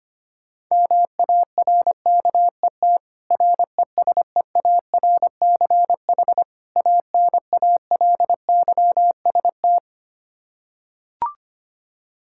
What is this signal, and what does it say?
Morse code, heard as a keyed tone, about 25 wpm, 700 Hz. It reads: MARKET RESEARCH ANALYST
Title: Practice QSO - Exchange 3 - More Info - 25wpm